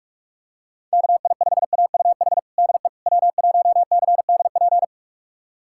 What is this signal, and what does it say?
Morse code, heard as a keyed tone, about 46 wpm, 690 Hz. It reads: KI5AUH DE W1CDP
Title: USA Callsign DE Callsign 45wpm